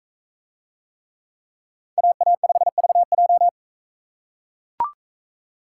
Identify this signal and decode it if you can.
Morse code, heard as a keyed tone, about 40 wpm, 700 Hz. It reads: AA5VJ